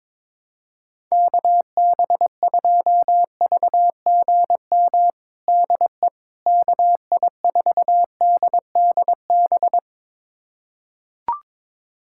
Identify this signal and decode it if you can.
Morse code, heard as a keyed tone, about 21 wpm, 700 Hz. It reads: KB2VGM DE KI4DDB
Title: USA Callsign DE Callsign - 6 Characters - 22wpm